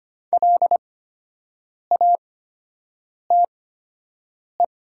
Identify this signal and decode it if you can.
Morse code, heard as a keyed tone, about 25 wpm, spaced as for 7 wpm, 700 Hz. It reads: LATE
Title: 8x Character Spacing - Top 300 Words - 25wpm